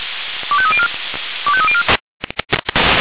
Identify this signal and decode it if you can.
Five-tone selective call (ZVEI), heard as quick runs of five tones, tones of about 70 ms. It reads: 25304 25304
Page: SDR/ZVEI